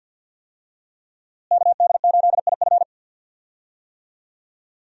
Morse code, heard as a keyed tone, about 52 wpm, 700 Hz. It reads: KD8IF
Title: USA Call Signs - 5 Characters - Repeated 2x - 50wpm